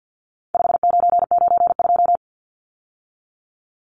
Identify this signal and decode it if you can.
Morse code, heard as a keyed tone, about 52 wpm, 700 Hz. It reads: 5992
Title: Numbers - 4-Digit 50wpm